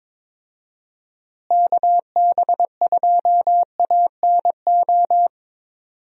Morse code, heard as a keyed tone, about 22 wpm, 700 Hz. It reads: KB2ANO
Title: USA Call Signs - 6 Characters - Repeated 3x - 22wpm